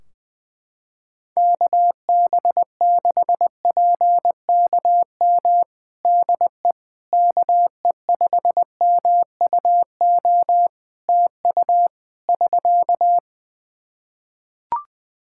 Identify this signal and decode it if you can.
Morse code, heard as a keyed tone, about 20 wpm, 700 Hz. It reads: KB6PKM DE KE5MUO TU <SK>
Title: Practice QSO - Exchange 4 - Ending - 20wpm